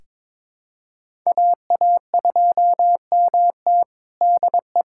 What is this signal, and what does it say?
Morse code, heard as a keyed tone, about 22 wpm, 700 Hz. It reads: AA2MT DE
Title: USA Callsign DE Callsign 22wpm